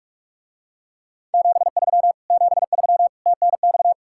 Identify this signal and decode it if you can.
Morse code, heard as a keyed tone, about 44 wpm, 700 Hz. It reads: 73 73 TNX